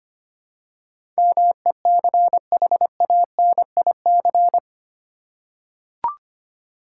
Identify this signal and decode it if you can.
Morse code, heard as a keyed tone, about 25 wpm, 700 Hz. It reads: MECHANIC
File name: Eight Letter Words - 25wpm